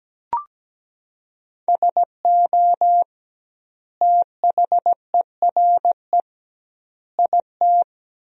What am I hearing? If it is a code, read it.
Morse code, heard as a keyed tone, about 17 wpm, 700 Hz. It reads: SO THERE IT
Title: 2x Word-Spacing - Sentences from Top 100 Words - 17wpm